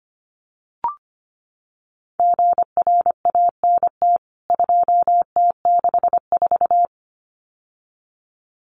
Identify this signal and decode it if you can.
Morse code, heard as a keyed tone, about 25 wpm, 700 Hz. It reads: GRANT 2T64
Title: CWT Exchange 25wpm